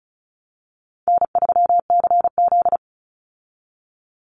Morse code, heard as a keyed tone, about 34 wpm, 700 Hz. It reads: N3CZ
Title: Call Signs Repeated 2x - 35wpm